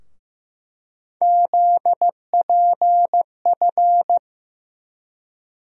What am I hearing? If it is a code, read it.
Morse code, heard as a keyed tone, about 16 wpm, 700 Hz. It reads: ZPF
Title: USA Call Sign Postfixes - 3 Letters 15wpm